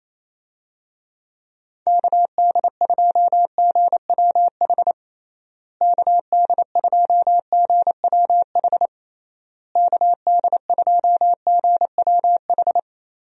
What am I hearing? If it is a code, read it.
Morse code, heard as a keyed tone, about 28 wpm, 700 Hz. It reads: KD2GWH KD2GWH KD2GWH
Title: USA Call Signs - 6 Characters - Repeated 3x - 28wpm